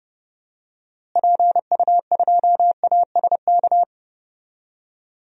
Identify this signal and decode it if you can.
Morse code, heard as a keyed tone, about 32 wpm, 700 Hz. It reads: PU2ASK
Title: DXCC-100 Call Signs - 6 Characters - Repeated 3x - 30wpm